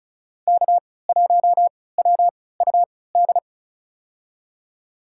Morse code, heard as a keyed tone, about 36 wpm, spaced as for 20 wpm, 700 Hz. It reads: K1WUD
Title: USA Call Signs - 5 Characters - 3x Character Spacing - 35wpm